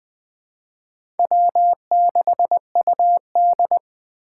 Morse code, heard as a keyed tone, about 20 wpm, 700 Hz. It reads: W6UD
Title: USA Call Signs - 4 Characters - 2x Character Spacing - 20wpm